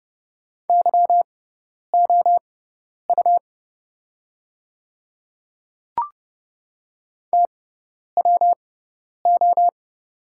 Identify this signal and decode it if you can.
Morse code, heard as a keyed tone, about 29 wpm, spaced as for 10 wpm, 690 Hz. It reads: YOU TWO
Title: 6x Character Spacing - Top 100 Words - 30wpm